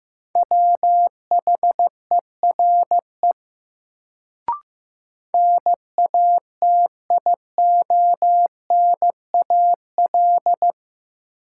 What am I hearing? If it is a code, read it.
Morse code, heard as a keyed tone, about 15 wpm, 700 Hz. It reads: WHERE NATIONAL
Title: Sets of 2 Words - New Word Drill - 15wpm